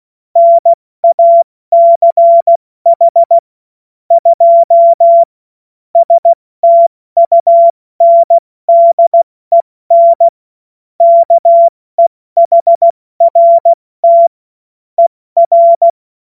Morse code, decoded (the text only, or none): NACH 2 STUNDEN KEHRT ER